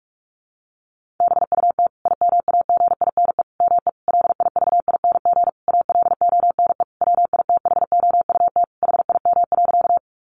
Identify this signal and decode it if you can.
Morse code, heard as a keyed tone, about 44 wpm, 700 Hz. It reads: BUT IMAGINE ME LIVING ALONE WITHOUT HIM.
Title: Sentences from Top 1400 Words 45wpm